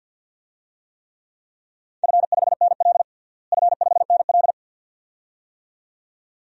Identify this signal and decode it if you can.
Morse code, heard as a keyed tone, about 53 wpm, 700 Hz. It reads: F5NL F5NL
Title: DXCC-100 Call Signs - 4 Characters - Repeated 2x - 50wpm